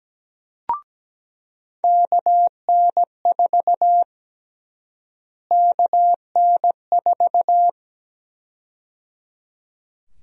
Morse code, heard as a keyed tone, about 17 wpm, 700 Hz. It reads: KN4 KN4
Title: USA Call Sign Prefixes - Repeated 2x - 17wpm